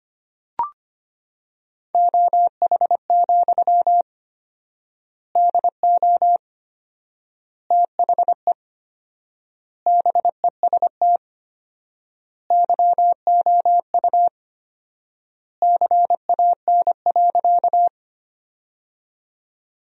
Morse code, heard as a keyed tone, about 25 wpm, 700 Hz. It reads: OH, DO THE BEST YOU CAN.
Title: 4x Word-Spacing - Sentences from Top 500 Words - 25wpm